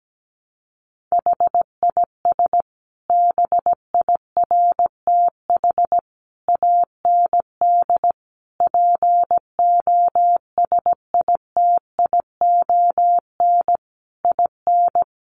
Morse code, heard as a keyed tone, about 17 wpm, 700 Hz. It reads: HIS BIRTH AND POSITION IN
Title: Sentences from Top 900 Words 17wpm